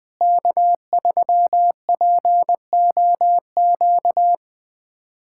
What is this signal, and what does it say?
Morse code, heard as a keyed tone, about 21 wpm, 700 Hz. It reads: K3POQ
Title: USA Callsign DE Callsign - 5 Characters - 20wpm